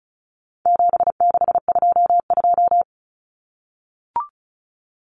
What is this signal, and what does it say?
Morse code, heard as a keyed tone, about 34 wpm, 700 Hz. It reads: 7622
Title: Common Cut Numbers - 4-Digit 35wpm